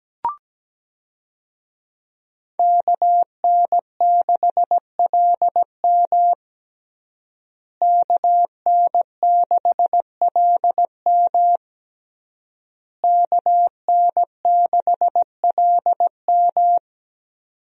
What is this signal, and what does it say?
Morse code, heard as a keyed tone, about 17 wpm, 700 Hz. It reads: KN6LM KN6LM KN6LM
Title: USA Call Signs - 5 Characters - Repeated 3x - 17wpm